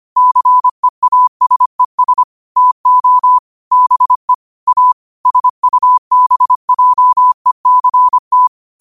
Morse code, heard as a keyed tone, about 25 wpm, 1000 Hz. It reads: CEASES TO BE A SUBJECT
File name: qotdfiles / 25wpm